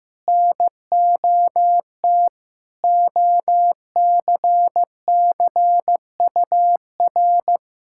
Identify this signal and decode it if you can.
Morse code, heard as a keyed tone, about 15 wpm, 700 Hz. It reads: NOT OCCUR
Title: Sentences from Top 1000 Words - Review Part 6 - 15wpm